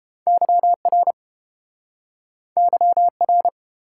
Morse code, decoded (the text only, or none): YR YR